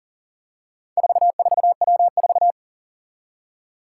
Morse code, heard as a keyed tone, about 40 wpm, 700 Hz. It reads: 44W4
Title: Four Challenging Letter-Number Combinations 40wpm